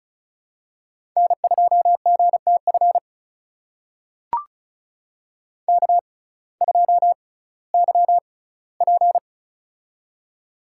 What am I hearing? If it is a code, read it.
Morse code, heard as a keyed tone, about 36 wpm, spaced as for 19 wpm, 700 Hz. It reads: N2GTF K2YP
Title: USA Call Signs - 6x Character Spacing - 35wpm